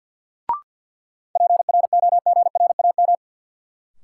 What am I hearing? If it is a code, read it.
Morse code, heard as a keyed tone, about 52 wpm, 700 Hz. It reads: PROGRAM
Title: Sentences from Top 1000 Words - Review Part 4 - New Word Drill - 50wpm